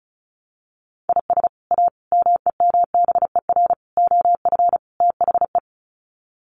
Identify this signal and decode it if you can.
Morse code, heard as a keyed tone, about 34 wpm, 700 Hz. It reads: IS A MEMBER OF THE
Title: Sets of 5 Words 35wpm